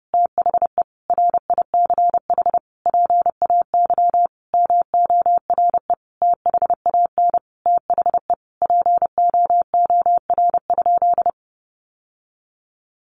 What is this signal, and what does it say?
Morse code, heard as a keyed tone, about 30 wpm, 700 Hz. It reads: THE RICH PAY MORE THAN THE POOR?
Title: Sentences from Top 1100 Words 30wpm